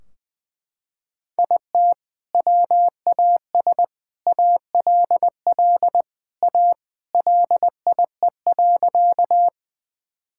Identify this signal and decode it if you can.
Morse code, heard as a keyed tone, about 20 wpm, 700 Hz. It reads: IT WAS ALL A LIE.
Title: Sentences from Top 1600 Words 20wpm